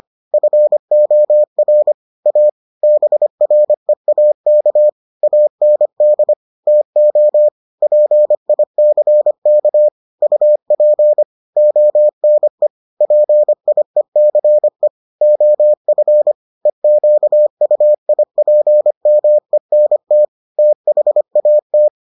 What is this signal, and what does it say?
Morse code, heard as a keyed tone, about 25 wpm, 600 Hz. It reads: FOR A BREAK AND TO PICK UP ONE PIECE OF EQUIPMENT THAT